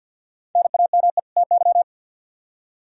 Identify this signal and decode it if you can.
Morse code, heard as a keyed tone, about 51 wpm, 700 Hz. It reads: NAME TY
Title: Practice QSO - Exchange 1 - Beginning - 50wpm